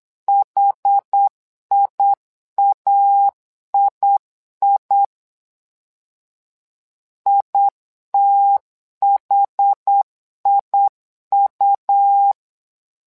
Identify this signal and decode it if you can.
Morse code, heard as a keyed tone, about 8 wpm, 800 Hz. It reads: HIAII ITHIU
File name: LEZIONI / 2 UHDN / 2 UH 8wpm 2129